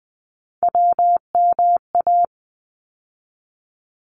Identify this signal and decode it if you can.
Morse code, heard as a keyed tone, about 20 wpm, 700 Hz. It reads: WMA
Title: ARRL Sections 20wpm